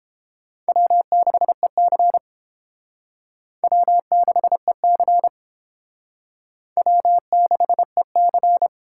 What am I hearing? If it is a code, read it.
Morse code, heard as a keyed tone, about 30 wpm, 700 Hz. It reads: W6EC W6EC W6EC